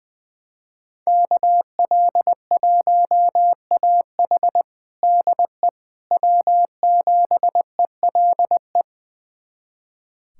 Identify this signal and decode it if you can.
Morse code, heard as a keyed tone, about 21 wpm, 700 Hz. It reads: KL1AH DE W7ELE
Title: USA Callsign DE Callsign 20wpm